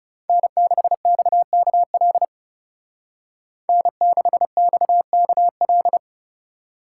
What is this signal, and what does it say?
Morse code, heard as a keyed tone, about 35 wpm, 700 Hz. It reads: N6XKL N6XKL
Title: Speed Racing - USA 1x3 Call Signs 30wpm